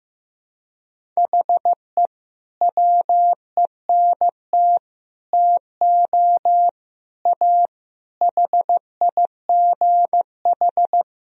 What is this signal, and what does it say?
Morse code, heard as a keyed tone, about 15 wpm, 700 Hz. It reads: HE WENT TO A HIGH
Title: Sentences from Top 1000 Words - Review Part 1 - 15wpm